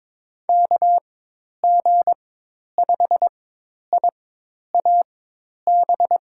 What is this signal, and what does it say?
Morse code, heard as a keyed tone, about 22 wpm, spaced as for 10 wpm, 700 Hz. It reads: KG5IAB
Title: USA Call Signs - 4x Character Spacing - 22wpm